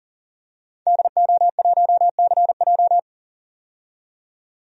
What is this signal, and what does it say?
Morse code, heard as a keyed tone, about 40 wpm, 700 Hz. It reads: DO1CJ
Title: DXCC-100 Call Signs - 5 Characters - 40wpm